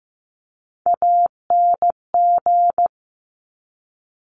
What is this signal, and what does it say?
Morse code, heard as a keyed tone, about 15 wpm, 700 Hz. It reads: ANG
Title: Top Three Letter Combinations in Words 15wpm